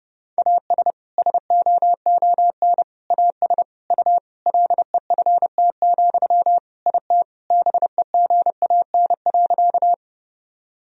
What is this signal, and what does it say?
Morse code, heard as a keyed tone, about 30 wpm, 700 Hz. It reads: AS SOON AS U LEFT, IT BEGAN.
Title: Sentences from Top 400 Words and Abbreviations 30wpm